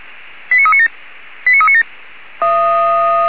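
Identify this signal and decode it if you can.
Five-tone selective call (ZVEI), heard as quick runs of five tones, tones of about 70 ms. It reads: 89397 89397
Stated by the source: ZVEI five-tone selective call received by SDR.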